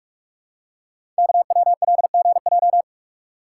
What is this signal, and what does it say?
Morse code, heard as a keyed tone, about 44 wpm, 690 Hz. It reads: KWLGJ
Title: Code Groups - 5 Letters - 45wpm